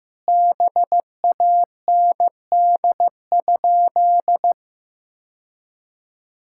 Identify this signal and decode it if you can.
Morse code, heard as a keyed tone, about 15 wpm, 700 Hz. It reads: BAND?
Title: Sentences from Top 1600 Words 15wpm